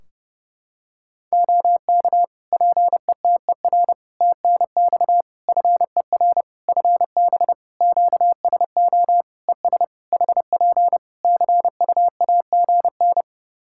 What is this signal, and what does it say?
Morse code, heard as a keyed tone, about 32 wpm, 700 Hz. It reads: OK PETER TNX FER FB QSO ES HP CUAGN
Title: K1USN SST QSO Practice 30wpm